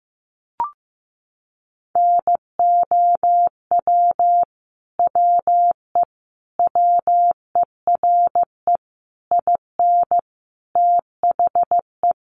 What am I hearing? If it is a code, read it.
Morse code, heard as a keyed tone, about 15 wpm, 700 Hz. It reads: NOW WE WERE IN THE
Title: Sentences from Top 1200 Words 15wpm